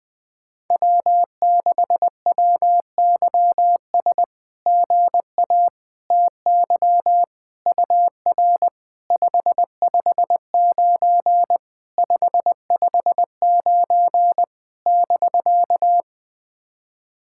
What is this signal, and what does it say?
Morse code, heard as a keyed tone, about 19 wpm, 690 Hz. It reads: W6WYS GA TY UR 559 559 <BK>
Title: SOTA QSO Practice 20wpm